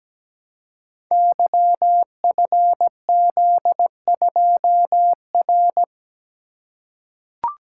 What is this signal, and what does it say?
Morse code, heard as a keyed tone, about 17 wpm, 700 Hz. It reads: YFZ2R
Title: Code Groups - 5 Letter:Numbers - 17wpm